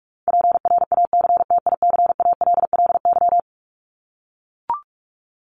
Morse code, heard as a keyed tone, about 44 wpm, 700 Hz. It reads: PRACTICALLY
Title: LY Suffix - Top 100 Words with Base Words 45wpm